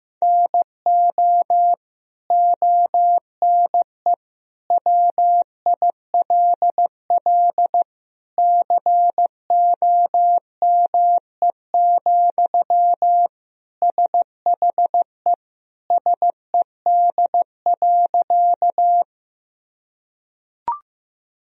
Morse code, heard as a keyed tone, about 15 wpm, 700 Hz. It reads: NO ONE WILL COME, SHE SED.
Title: Sentences from Top 100 Words and Abbreviations 15wpm